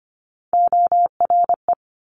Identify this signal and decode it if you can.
Morse code, heard as a keyed tone, about 25 wpm, 700 Hz. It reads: ORE
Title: Multi-letter Phonemes Quick Fire 25wpm